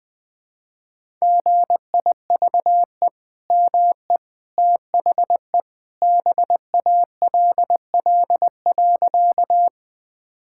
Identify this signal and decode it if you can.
Morse code, heard as a keyed tone, about 21 wpm, 700 Hz. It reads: GIVE ME THE BALL.
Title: Sentences from Top 1000 Words - Review Part 6 - 20wpm